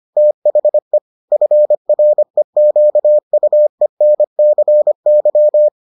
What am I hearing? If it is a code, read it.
Morse code, heard as a keyed tone, about 25 wpm, 600 Hz. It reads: THE FREQUENCY